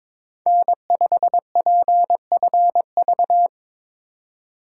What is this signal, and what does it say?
Morse code, heard as a keyed tone, about 22 wpm, 700 Hz. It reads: N5PFV
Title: USA Call Signs - 2x Character Spacing - 22wpm